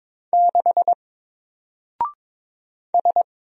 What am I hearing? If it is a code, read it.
Morse code, heard as a keyed tone, about 22 wpm, 700 Hz. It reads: 6 S
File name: Single-Letter-Number-22wpm